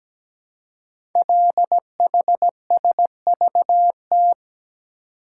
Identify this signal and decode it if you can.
Morse code, heard as a keyed tone, about 17 wpm, 700 Hz. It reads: LHSVT
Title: Code Groups - 5 Letters - 17wpm